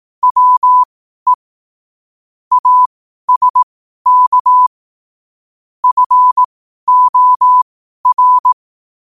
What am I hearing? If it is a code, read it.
Morse code, heard as a keyed tone, about 18 wpm, spaced as for 12 wpm, 1000 Hz. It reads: WE ASK FOR
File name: qotdfiles / 12wpm